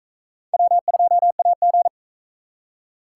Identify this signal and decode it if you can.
Morse code, heard as a keyed tone, about 41 wpm, 700 Hz. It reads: W2AG